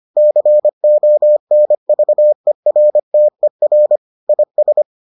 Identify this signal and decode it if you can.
Morse code, heard as a keyed tone, about 25 wpm, 600 Hz. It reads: CONVERTER IS